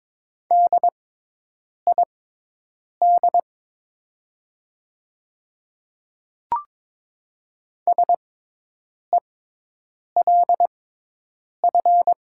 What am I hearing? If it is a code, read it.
Morse code, heard as a keyed tone, about 22 wpm, spaced as for 8 wpm, 700 Hz. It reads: DID SELF
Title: 6x Character Spacing - Top 200 Words - 22wpm